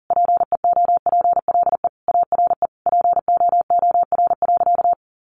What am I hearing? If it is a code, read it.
Morse code, heard as a keyed tone, about 40 wpm, 700 Hz. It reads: PEOPLE ARE POOR.
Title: Sentences from Top 1100 Words 40wpm